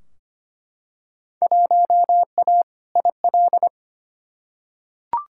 Morse code, heard as a keyed tone, about 25 wpm, 700 Hz. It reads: 1A IL
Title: Field Day Exchanges Encore - 25wpm